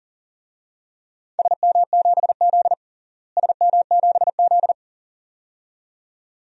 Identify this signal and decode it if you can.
Morse code, heard as a keyed tone, about 40 wpm, 700 Hz. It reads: SM7Z SM7Z
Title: DXCC-100 Call Signs - 4 Characters - Repeated 2x - 40wpm